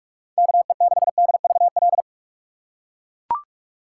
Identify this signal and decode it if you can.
Morse code, heard as a keyed tone, about 41 wpm, 700 Hz. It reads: KE6DVL